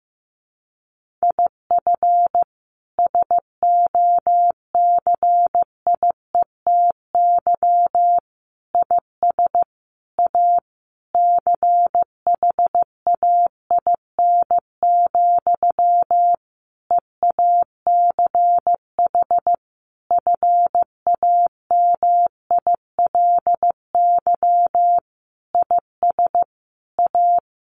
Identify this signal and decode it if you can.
Morse code, heard as a keyed tone, about 15 wpm, 700 Hz. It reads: IF SOCIETY IS A CHAIN, EACH FAMILY IS A
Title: Sentences from Top 1500 Words 15wpm